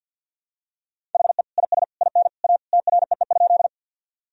Morse code, heard as a keyed tone, about 47 wpm, 700 Hz. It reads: HE IS IN A TREE?
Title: Sentences from Top 1000 Words - Review Part 1 - 50wpm